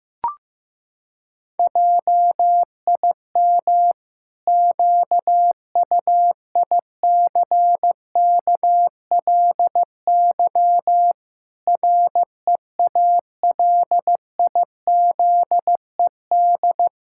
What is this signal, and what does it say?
Morse code, heard as a keyed tone, about 15 wpm, 700 Hz. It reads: JIM QUICKLY REALIZED
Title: Pangram Challenge v2 - 15wpm